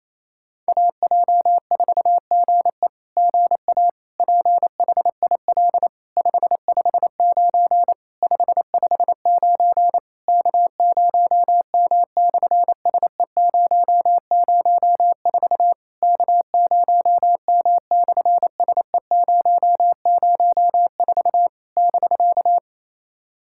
Morse code, heard as a keyed tone, about 28 wpm, 700 Hz. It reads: AJ4GE GA PHIL 559 559 K0M/SE004 K0M/SE004 <BK>
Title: SOTA QSO Practice 28wpm